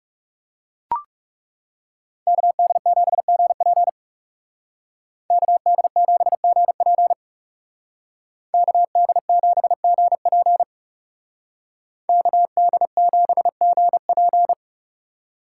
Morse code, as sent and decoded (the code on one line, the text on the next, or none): -.- -.. --... --. .--. / -.- -.. --... --. .--. / -.- -.. --... --. .--. / -.- -.. --... --. .--.
KD7GP KD7GP KD7GP KD7GP